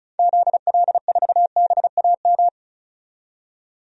Morse code, heard as a keyed tone, about 36 wpm, 690 Hz. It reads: ZL4BAM
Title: DXCC-100 Call Signs - 6 Characters - 35wpm